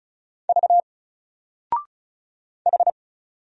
Morse code, heard as a keyed tone, about 34 wpm, 700 Hz. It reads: V H
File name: Single-Letter-Number-35wpm